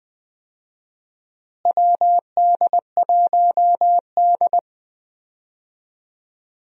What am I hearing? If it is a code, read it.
Morse code, heard as a keyed tone, about 20 wpm, 700 Hz. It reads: WD1D
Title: USA Call Signs - 4 Characters - Repeated 2x - 20wpm